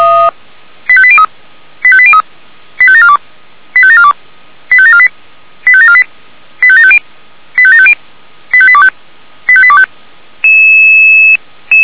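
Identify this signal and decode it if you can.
Five-tone selective call (ZVEI), heard as quick runs of five tones, tones of about 70 ms. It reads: 85803 85803 85742 85742 85748 85748 85750 85750 85825 85825
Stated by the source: Received by SDR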